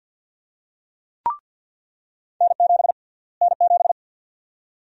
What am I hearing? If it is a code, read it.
Morse code, heard as a keyed tone, about 46 wpm, 700 Hz. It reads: N7 N7
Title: USA Call Sign Prefixes - Repeated 2x - 50wpm